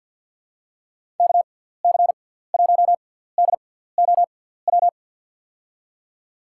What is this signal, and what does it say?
Morse code, heard as a keyed tone, about 47 wpm, spaced as for 17 wpm, 700 Hz. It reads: KC1DOW
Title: USA Call Signs - 6 Characters - 6x Character Spacing - 50wpm